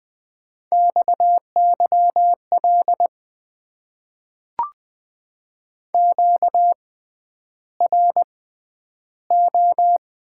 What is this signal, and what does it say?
Morse code, heard as a keyed tone, about 20 wpm, spaced as for 10 wpm, 700 Hz. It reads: XYL QRO
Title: Top 90 QSO Elements - 6x Character Spacing - 20wpm